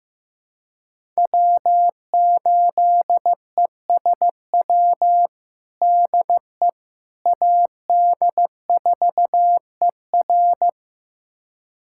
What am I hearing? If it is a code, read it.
Morse code, heard as a keyed tone, about 15 wpm, 700 Hz. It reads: W8ESW DE AD4ER
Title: USA Callsign DE Callsign - 5 Characters - 15wpm